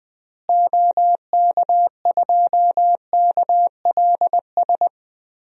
Morse code, heard as a keyed tone, about 20 wpm, 700 Hz. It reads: OK2KLS
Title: DXCC-100 Call Signs - 6 Characters - Repeated 3x - 20wpm